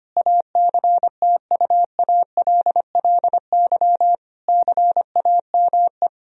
Morse code, heard as a keyed tone, about 25 wpm, 700 Hz. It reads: ACTUALLY CAME
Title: Sentences from Top 1000 Words - Review Part 4 - 25wpm